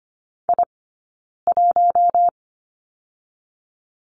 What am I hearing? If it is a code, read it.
Morse code, heard as a keyed tone, about 25 wpm, spaced as for 9 wpm, 700 Hz. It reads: I1
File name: Lesson 15d - Number 5 - Two Characters 25wpm